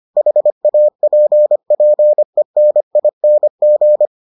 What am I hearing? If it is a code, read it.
Morse code, heard as a keyed tone, about 25 wpm, 600 Hz. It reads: HAPPENING